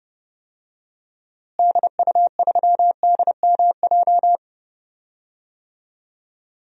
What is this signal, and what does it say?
Morse code, heard as a keyed tone, about 29 wpm, 700 Hz. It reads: DU3DMJ
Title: DXCC-100 Call Signs - 6 Characters - Repeated 2x - 30wpm